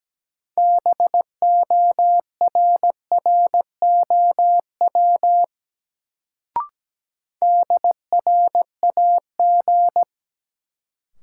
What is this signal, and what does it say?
Morse code, heard as a keyed tone, about 17 wpm, 700 Hz. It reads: BORROW DRAG
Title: Sentences from Top 1500 Words - New Word Drill Alt - 17wpm